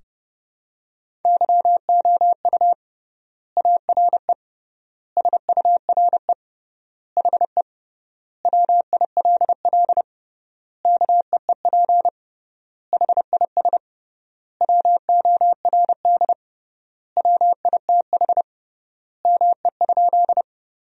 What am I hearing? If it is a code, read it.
Morse code, heard as a keyed tone, about 30 wpm, 700 Hz. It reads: YOU ARE SURE HE WILL KEEP HIS WORD WITH ME?
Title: 3x Word-Spacing - Sentences from Top 300 Words - 30wpm